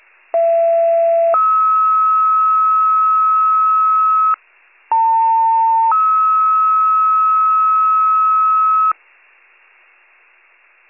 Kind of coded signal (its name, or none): two-tone page